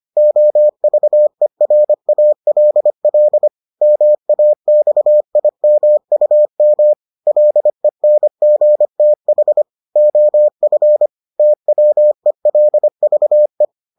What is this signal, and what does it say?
Morse code, heard as a keyed tone, about 25 wpm, 600 Hz. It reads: OVERALL MAXIMUM LENGTH OF TWELVE